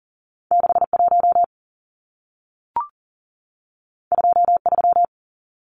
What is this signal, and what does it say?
Morse code, heard as a keyed tone, about 40 wpm, 700 Hz. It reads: 61 23
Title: Numbers - 2-Digit 40wpm